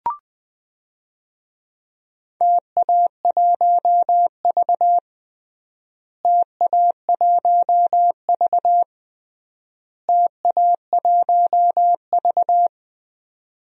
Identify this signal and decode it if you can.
Morse code, heard as a keyed tone, about 20 wpm, 700 Hz. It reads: TA1V TA1V TA1V
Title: DXCC-100 Call Signs - 4 Characters - Repeated 3x - 20wpm